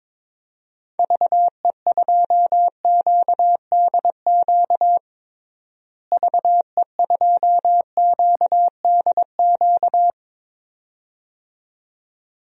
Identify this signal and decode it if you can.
Morse code, heard as a keyed tone, about 22 wpm, 700 Hz. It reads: VE2QDQ VE2QDQ
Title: DXCC-100 Call Signs - 6 Characters - Repeated 2x - 22wpm